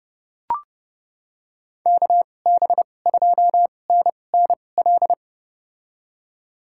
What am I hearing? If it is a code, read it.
Morse code, heard as a keyed tone, about 30 wpm, spaced as for 22 wpm, 700 Hz. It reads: KB2NNL
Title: USA Call Signs - 6 Characters - 2x Character Spacing - 30wpm